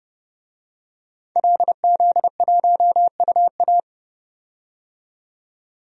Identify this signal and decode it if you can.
Morse code, heard as a keyed tone, about 30 wpm, 700 Hz. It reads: LZ1UA
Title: DXCC-100 Call Signs - 5 Characters - 30wpm